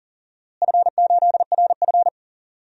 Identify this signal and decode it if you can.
Morse code, heard as a keyed tone, about 37 wpm, 700 Hz. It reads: F8RF
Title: DXCC-100 Call Signs - 4 Characters - Repeated 3x - 40wpm